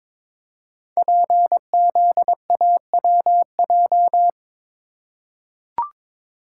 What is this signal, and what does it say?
Morse code, heard as a keyed tone, about 22 wpm, 700 Hz. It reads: PZAWJ
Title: Code Groups - 5 Letter:Numbers - 22wpm